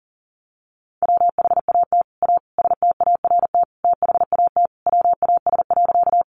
Morse code, heard as a keyed tone, about 40 wpm, 700 Hz. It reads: WHAT A START THAT WAS.
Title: Sentences from Top 300 Words 40wpm